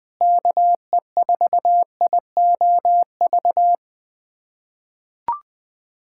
Morse code, heard as a keyed tone, about 21 wpm, 700 Hz. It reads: KE4IOV
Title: USA Callsign DE Callsign - 6 Characters - 20wpm